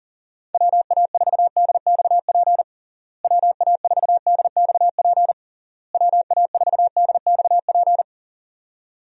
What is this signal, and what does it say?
Morse code, heard as a keyed tone, about 40 wpm, 700 Hz. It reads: WA4DXP WA4DXP WA4DXP
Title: USA Call Signs - 6 Characters - Repeated 3x - 40wpm